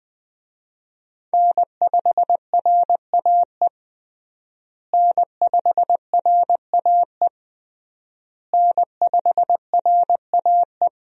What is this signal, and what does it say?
Morse code, heard as a keyed tone, about 20 wpm, 700 Hz. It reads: N5RAE N5RAE N5RAE
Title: USA Call Signs - 5 Characters - Repeated 3x - 20wpm